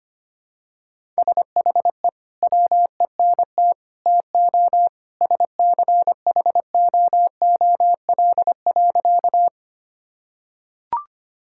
Morse code, heard as a keyed tone, about 25 wpm, 700 Hz. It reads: SHE WENT TO SCHOOL.
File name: practice / 25wpm / Sentences with 2-Word Phrases - 25wpm